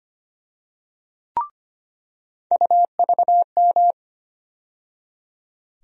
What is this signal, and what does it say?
Morse code, heard as a keyed tone, about 25 wpm, 700 Hz. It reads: UVM
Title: USA Call Sign Postfixes - 3 Letters 25wpm